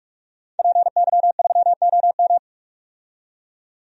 Morse code, heard as a keyed tone, about 46 wpm, 700 Hz. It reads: PY3OM
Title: DXCC-100 Call Signs - 5 Characters - 45wpm